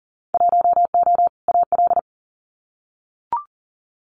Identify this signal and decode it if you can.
Morse code, heard as a keyed tone, about 40 wpm, 700 Hz. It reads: 1O AL
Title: Winter Field Day Exchanges 40wpm